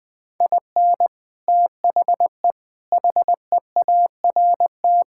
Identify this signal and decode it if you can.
Morse code, heard as a keyed tone, about 20 wpm, 700 Hz. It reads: IN THE HEART
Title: Sentences with 3-Word Phrases - 20wpm